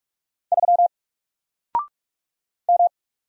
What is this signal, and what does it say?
Morse code, heard as a keyed tone, about 46 wpm, 700 Hz. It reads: 3 M
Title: Single Letter-Number 45wpm